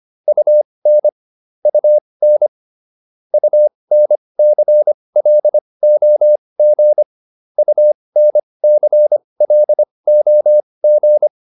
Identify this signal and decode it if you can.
Morse code, heard as a keyed tone, about 25 wpm, spaced as for 20 wpm, 600 Hz. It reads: UN UN UNCLOG UNCLOG